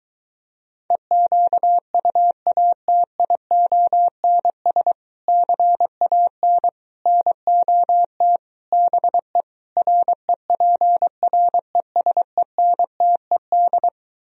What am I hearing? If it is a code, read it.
Morse code, heard as a keyed tone, about 23 wpm, 700 Hz. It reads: EQUATIONS CAN NOT BE REPRESENTED